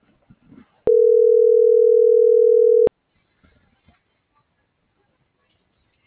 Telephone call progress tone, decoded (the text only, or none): ringback tone